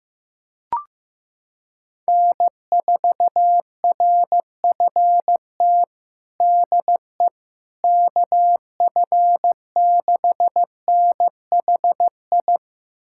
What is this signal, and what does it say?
Morse code, heard as a keyed tone, about 15 wpm, 700 Hz. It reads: N4RFT DE KF6NHI
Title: USA Callsign DE Callsign 15wpm